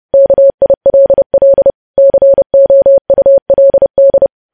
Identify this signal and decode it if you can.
Morse code, heard as a keyed tone, about 30 wpm, 550 Hz. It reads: KILL COULD
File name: podcast / News headlines-1x-30x30